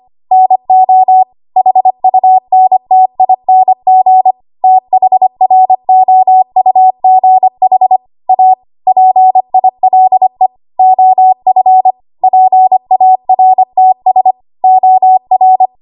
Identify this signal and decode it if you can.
Morse code, heard as a keyed tone, about 25 wpm, 750 Hz. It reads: NO HUNTING THROUGH A PILE OF PARTS OR
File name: Morse / 25WPM